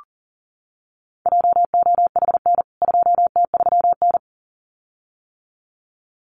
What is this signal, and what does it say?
Morse code, heard as a keyed tone, about 40 wpm, 700 Hz. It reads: JOHN 2T3N
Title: CWT Exchange 40wpm